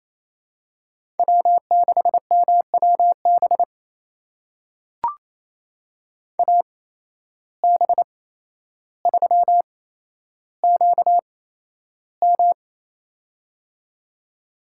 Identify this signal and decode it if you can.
Morse code, heard as a keyed tone, about 28 wpm, spaced as for 12 wpm, 700 Hz. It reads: W6MWB AB3QM
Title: USA Call Signs - 5 Characters - 8x Character Spacing - 28wpm